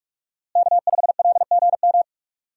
Morse code, heard as a keyed tone, about 42 wpm, 700 Hz. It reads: K5LGM
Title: USA Call Signs - 5 Characters - Repeated 2x - 45wpm